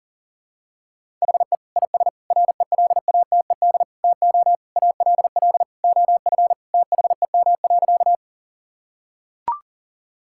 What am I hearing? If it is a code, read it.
Morse code, heard as a keyed tone, about 40 wpm, 700 Hz. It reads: HE IS RELATED TO ALL OF THEM.
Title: Sentences from Top 1000 Words - Review Part 5 - 40wpm